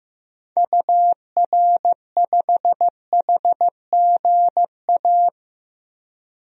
Morse code, heard as a keyed tone, about 16 wpm, 700 Hz. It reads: UR5HGA
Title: DXCC-100 Call Signs - 6 Characters - 15wpm